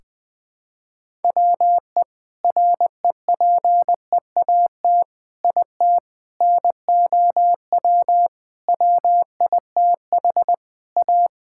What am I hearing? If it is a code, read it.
Morse code, heard as a keyed tone, about 21 wpm, 700 Hz. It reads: WE REPEAT IT NOW WITH A
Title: Sentences from Top 1600 Words 20wpm